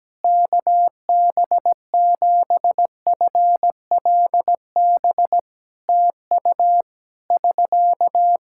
Morse code, heard as a keyed tone, about 17 wpm, 700 Hz. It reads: KB7FLB TU <SK>
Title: Practice QSO - Exchange 4 - Ending - 17wpm